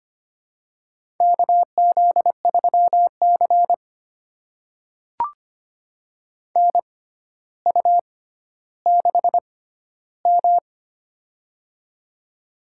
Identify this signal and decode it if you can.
Morse code, heard as a keyed tone, about 25 wpm, spaced as for 13 wpm, 700 Hz. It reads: KZ3C NU6M
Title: USA Call Signs - 4 Characters - 6x Character Spacing - 25wpm